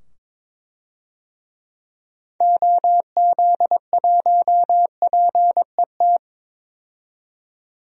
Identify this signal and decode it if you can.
Morse code, heard as a keyed tone, about 22 wpm, 700 Hz. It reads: OZ1PET
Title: DXCC-100 Call Signs - 6 Characters - 22wpm